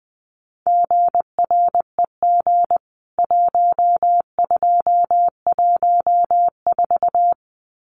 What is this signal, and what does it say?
Morse code, heard as a keyed tone, about 20 wpm, 700 Hz. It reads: GREG 1214
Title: CWT Exchange 20wpm